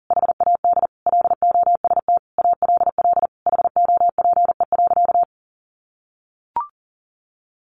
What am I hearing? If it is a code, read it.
Morse code, heard as a keyed tone, about 37 wpm, 700 Hz. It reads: HAD LOST ALL HOPE.
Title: Sentences from Top 1400 Words 40wpm